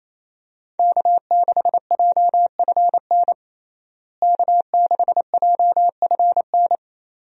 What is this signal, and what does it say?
Morse code, heard as a keyed tone, about 27 wpm, 700 Hz. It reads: K6JFN K6JFN
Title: USA Call Signs - 5 Characters - Repeated 2x - 28wpm